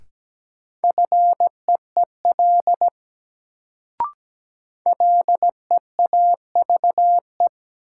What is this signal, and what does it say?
Morse code, heard as a keyed tone, about 17 wpm, 700 Hz. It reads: FEEL LEAVE
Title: Sentences from Top 400 Words - New Word Drill - 17wpm